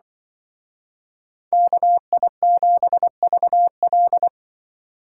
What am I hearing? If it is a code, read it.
Morse code, heard as a keyed tone, about 24 wpm, 700 Hz. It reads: KI7VL
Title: Speed Racing - USA 2x2 Call Signs 20wpm